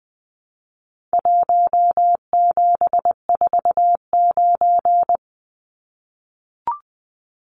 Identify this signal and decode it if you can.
Morse code, heard as a keyed tone, about 20 wpm, 700 Hz. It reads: 1749
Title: Numbers - Contest Style - 4-Digit 20wpm